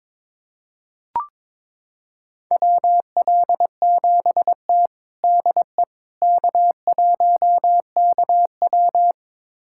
Morse code, heard as a keyed tone, about 22 wpm, 700 Hz. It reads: WL7T DE K1KW
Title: USA Callsign DE Callsign - 4 Characters - 22wpm